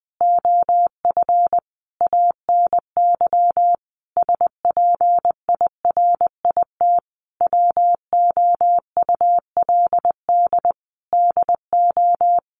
Morse code, heard as a keyed tone, about 20 wpm, 700 Hz. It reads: OF ANY SPIRIT WOULD DO
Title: Sentences from Top 1400 Words 20wpm